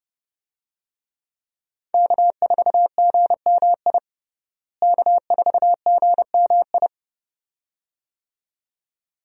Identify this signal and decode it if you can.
Morse code, heard as a keyed tone, about 30 wpm, 700 Hz. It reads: K4GMI K4GMI
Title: USA Call Signs - 5 Characters - Repeated 2x - 30wpm